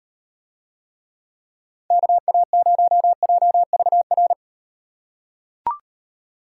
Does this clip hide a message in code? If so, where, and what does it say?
Yes: Morse code from 1.9 to 4.3 s, heard as a keyed tone, about 38 wpm, 700 Hz. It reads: KA0JVR